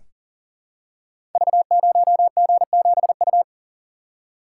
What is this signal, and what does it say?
Morse code, heard as a keyed tone, about 38 wpm, 700 Hz. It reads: V0G7U
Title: Code Groups - 5 Letter:Numbers - 40wpm